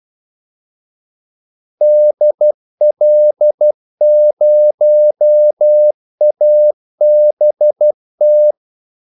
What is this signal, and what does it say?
Morse code, heard as a keyed tone, about 12 wpm, 600 Hz. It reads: DL0ABT
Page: 2 ekstra ordmellemrum mellem kaldesignalerne